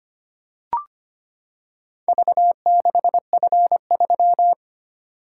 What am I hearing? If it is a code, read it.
Morse code, heard as a keyed tone, about 25 wpm, 700 Hz. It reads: V6F3
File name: practice / 25wpm / Four Letter-Number Combinations - 25wpm